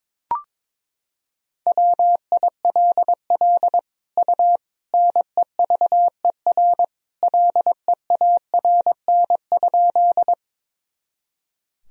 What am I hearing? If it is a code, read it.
Morse code, heard as a keyed tone, about 22 wpm, 700 Hz. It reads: WILL U NEVER LEARN?
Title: Sentences from Top 300 Words and Abbreviations 22wpm